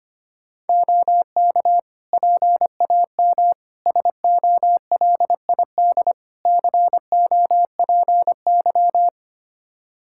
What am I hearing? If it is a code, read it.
Morse code, heard as a keyed tone, about 25 wpm, 700 Hz. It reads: OK PAM SOLID COPY
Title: Practice QSO - Exchange 3 - More Info - 25wpm